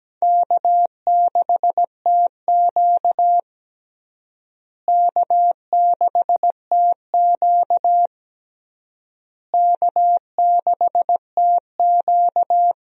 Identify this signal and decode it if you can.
Morse code, heard as a keyed tone, about 17 wpm, 700 Hz. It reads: K6TQ K6TQ K6TQ
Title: USA Call Signs - 4 Characters - Repeated 3x - 17wpm